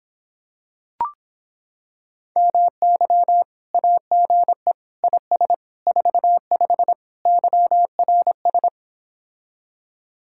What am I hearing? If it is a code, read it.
Morse code, heard as a keyed tone, about 26 wpm, 700 Hz. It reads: MY AGE IS 45 YRS